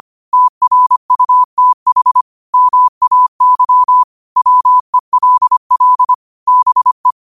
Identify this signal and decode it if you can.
Morse code, heard as a keyed tone, about 25 wpm, 1000 Hz. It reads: TRUTH MAY WELL BE
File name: qotdfiles / 25wpm